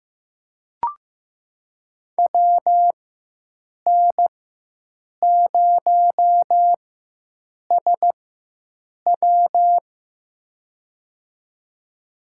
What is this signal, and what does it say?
Morse code, heard as a keyed tone, about 15 wpm, spaced as for 7 wpm, 700 Hz. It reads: WN0SW
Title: USA Call Signs - 5 Characters - 4x Character Spacing - 15wpm